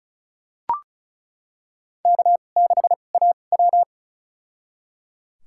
Morse code, heard as a keyed tone, about 36 wpm, spaced as for 26 wpm, 700 Hz. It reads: K6AW
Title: USA Call Signs - 4 Characters - 2x Character Spacing - 35wpm